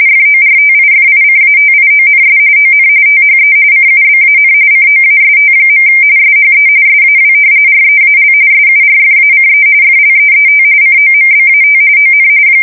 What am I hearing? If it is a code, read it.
RTTY, about 45.45 baud. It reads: FORTIFICATIONS ON THE CUBAN COAST. SHE SAW REAL SURFACE FLEET COMBAT ON
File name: fsk170 / battleships